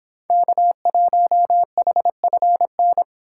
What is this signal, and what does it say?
Morse code, heard as a keyed tone, about 25 wpm, 700 Hz. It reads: K1HFN